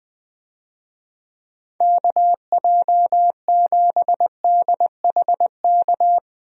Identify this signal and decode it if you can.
Morse code, heard as a keyed tone, about 20 wpm, 700 Hz. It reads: KJ7DHK
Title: USA Call Signs - 6 Characters - Repeated 2x - 20wpm